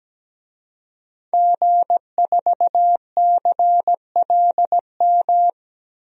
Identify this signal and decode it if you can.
Morse code, heard as a keyed tone, about 17 wpm, 700 Hz. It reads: G4CLM
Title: DXCC-100 Call Signs - 5 Characters - 17wpm